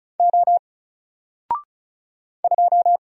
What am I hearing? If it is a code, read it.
Morse code, heard as a keyed tone, about 35 wpm, 700 Hz. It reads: O 2
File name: Single-Letter-Number-35wpm